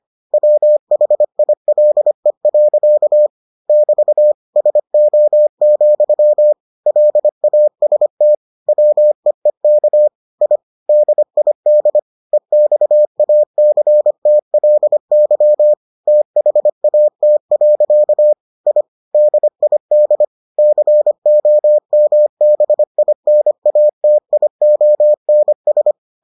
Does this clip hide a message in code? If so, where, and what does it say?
Yes: Morse code from 0.3 to 25.9 s, heard as a keyed tone, about 25 wpm, 600 Hz. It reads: WHILE. <BT> SO, LAST WEEK I DID EXACTLY THAT. I DID COMBINATIONS